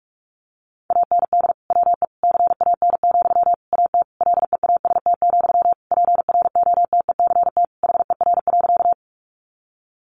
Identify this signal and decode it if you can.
Morse code, heard as a keyed tone, about 44 wpm, 700 Hz. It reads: AND WE CAN, AT LEAST, PROTECT HER.
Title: Sentences from Top 900 Words 45wpm